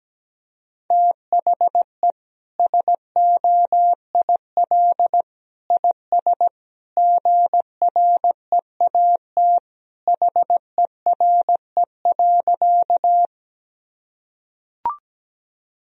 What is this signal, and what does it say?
Morse code, heard as a keyed tone, about 17 wpm, 700 Hz. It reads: THE SOIL IS GREAT HERE.
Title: Sentences from Top 1000 Words - Review Part 4 - 17wpm